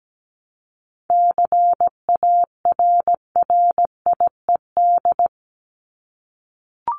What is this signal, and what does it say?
Morse code, heard as a keyed tone, about 17 wpm, 700 Hz. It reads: CARRIED
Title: ED Suffix - Top 100 Words with Base Words 17wpm